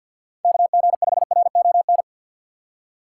Morse code, heard as a keyed tone, about 51 wpm, 700 Hz. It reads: KG5RON
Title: USA Call Signs - 6 Characters - 4x Character Spacing - 50wpm